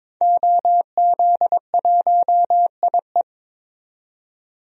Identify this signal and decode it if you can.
Morse code, heard as a keyed tone, about 22 wpm, 700 Hz. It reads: OZ1IE
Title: DXCC-100 Call Signs - 5 Characters - 22wpm